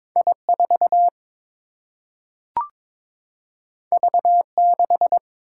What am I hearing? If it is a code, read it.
Morse code, heard as a keyed tone, about 22 wpm, 700 Hz. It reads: I4 V6
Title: Single-Letter Single-Number 22wpm